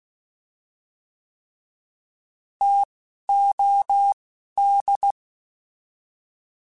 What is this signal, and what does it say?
Morse code, heard as a keyed tone, about 16 wpm, 800 Hz. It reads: T O D